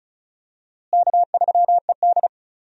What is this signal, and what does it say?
Morse code, heard as a keyed tone, about 36 wpm, 700 Hz. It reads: K3ED
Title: USA Call Signs - 4 Characters - Repeated 3x - 35wpm